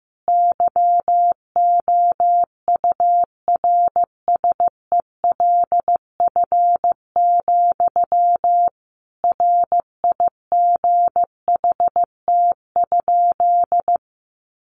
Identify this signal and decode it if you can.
Morse code, heard as a keyed tone, about 14 wpm, 700 Hz. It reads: YOURSELF, RIGHT?
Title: Sentences from Top 1200 Words 15wpm